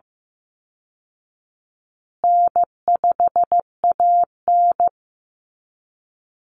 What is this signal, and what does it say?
Morse code, heard as a keyed tone, about 15 wpm, 700 Hz. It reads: N5AN
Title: Call Signs Repeated 2x - 15wpm